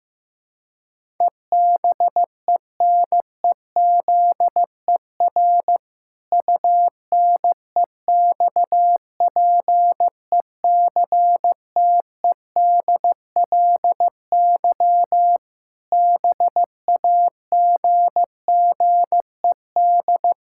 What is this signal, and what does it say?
Morse code, heard as a keyed tone, about 15 wpm, 700 Hz. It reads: EBENEZER UNEXPECTEDLY BAGGED
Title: Pangram Challenge v2 - 15wpm